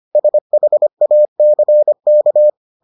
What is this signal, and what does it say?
Morse code, heard as a keyed tone, about 25 wpm, 600 Hz. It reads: SHACK